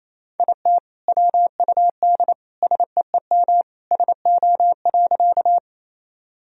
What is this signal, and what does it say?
Morse code, heard as a keyed tone, about 28 wpm, 700 Hz. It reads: IT WUD SEEM SO.
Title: Sentences from Top 300 Words and Abbreviations 28wpm